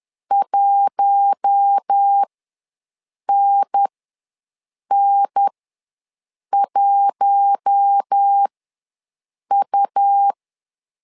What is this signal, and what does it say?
Morse code, heard as a keyed tone, about 10 wpm, spaced as for 6 wpm, 800 Hz. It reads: 1NN1U
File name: LEZIONI / 4 Q1 ws 10wpm 2538